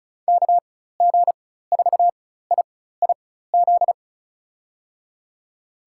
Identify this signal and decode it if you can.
Morse code, heard as a keyed tone, about 34 wpm, spaced as for 16 wpm, 700 Hz. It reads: KG4IIZ
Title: USA Call Signs - 6 Characters - 4x Character Spacing - 35wpm